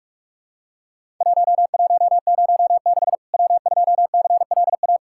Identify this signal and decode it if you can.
Morse code, heard as a keyed tone, about 44 wpm, 700 Hz. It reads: 1106 W2CLA
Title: Worked All Europe DX Contest - QTC Exchange - 45wpm